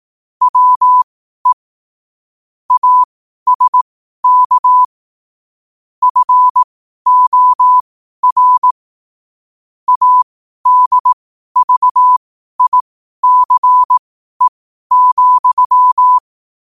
Morse code, heard as a keyed tone, about 18 wpm, spaced as for 12 wpm, 1000 Hz. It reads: WE ASK FOR ADVICE,
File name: qotdfiles / 12wpm